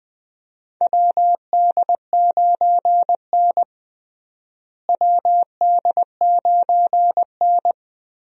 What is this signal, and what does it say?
Morse code, heard as a keyed tone, about 20 wpm, 700 Hz. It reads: WD9N WD9N
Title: USA Call Signs - 4 Characters - Repeated 3x - 20wpm